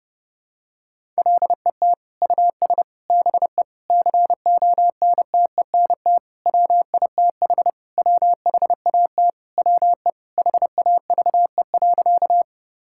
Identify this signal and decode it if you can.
Morse code, heard as a keyed tone, about 30 wpm, 700 Hz. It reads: LET US BE CONTENT WITH WHAT WE HAVE.
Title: Sentences from Top 1000 Words - Review Part 6 - 30wpm